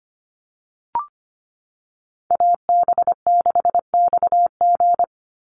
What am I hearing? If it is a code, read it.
Morse code, heard as a keyed tone, about 25 wpm, 700 Hz. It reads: AB6XG
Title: Call Signs - 25wpm